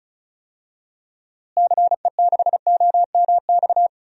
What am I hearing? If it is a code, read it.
Morse code, heard as a keyed tone, about 34 wpm, 700 Hz. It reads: CE6OMX
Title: DXCC-100 Call Signs - 6 Characters - Repeated 2x - 35wpm